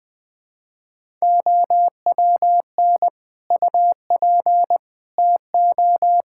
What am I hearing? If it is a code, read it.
Morse code, heard as a keyed tone, about 20 wpm, 700 Hz. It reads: OWN UP TO
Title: Difficult Sentences with Hints - 20wpm